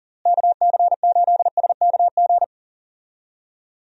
Morse code, heard as a keyed tone, about 40 wpm, 700 Hz. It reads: KC8SKG
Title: USA Call Signs - 6 Characters - Repeated 2x - 40wpm